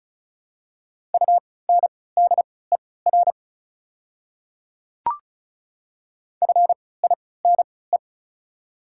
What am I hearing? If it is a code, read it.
Morse code, heard as a keyed tone, about 36 wpm, spaced as for 20 wpm, 690 Hz. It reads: UNDER FINE
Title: 3x Character Spacing - Top 500 Words - 35wpm